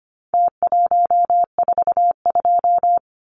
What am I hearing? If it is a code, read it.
Morse code, heard as a keyed tone, about 25 wpm, 700 Hz. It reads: T142
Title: Numbers - Contest Style - 4-Digit 25wpm